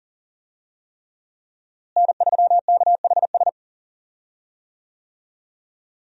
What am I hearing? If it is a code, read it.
Morse code, heard as a keyed tone, about 37 wpm, 700 Hz. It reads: N3KHS
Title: USA Call Signs - 5 Characters - Repeated 3x - 40wpm